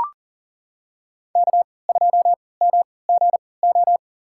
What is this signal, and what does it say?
Morse code, heard as a keyed tone, about 40 wpm, spaced as for 23 wpm, 700 Hz. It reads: K2MGO
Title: USA Call Signs - 3x Character Spacing - 40wpm